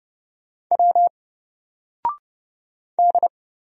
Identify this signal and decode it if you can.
Morse code, heard as a keyed tone, about 30 wpm, 700 Hz. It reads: W D
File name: Single-Letter-Number-30wpm